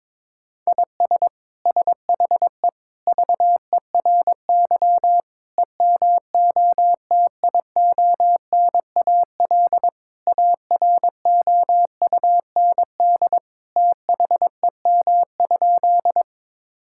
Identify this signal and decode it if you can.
Morse code, heard as a keyed tone, about 22 wpm, 700 Hz. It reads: IS SHE VERY EMOTIONAL AROUND THEM?
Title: Sentences from Top 1000 Words - Review Part 5 - 22wpm